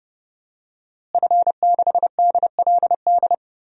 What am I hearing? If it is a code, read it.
Morse code, heard as a keyed tone, about 31 wpm, 700 Hz. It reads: F6DLD
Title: DXCC-100 Call Signs - 5 Characters - Repeated 3x - 30wpm